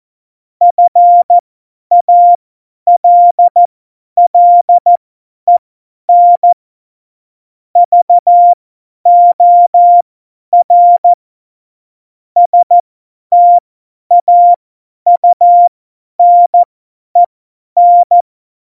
Morse code, heard as a keyed tone, about 14 wpm, spaced as for 10 wpm, 700 Hz. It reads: FALLEN VOR STAUNEN
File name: Morsekurs / Witze / Witz 13 March 2018, 50 F70